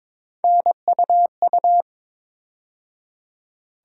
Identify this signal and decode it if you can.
Morse code, heard as a keyed tone, about 21 wpm, 700 Hz. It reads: NUU
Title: USA Call Sign Postfixes - 3 Letters 22wpm